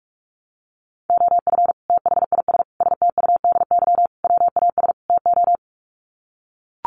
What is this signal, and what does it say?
Morse code, heard as a keyed tone, about 44 wpm, 700 Hz. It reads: OF THIS STUDY WAS TO
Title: Sets of 5 Words 45wpm